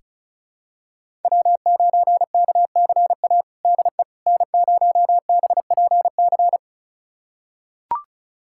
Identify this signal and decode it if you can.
Morse code, heard as a keyed tone, about 36 wpm, 690 Hz. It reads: W9KCA DE N0BPC
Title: USA Callsign DE Callsign - 5 Characters - 35wpm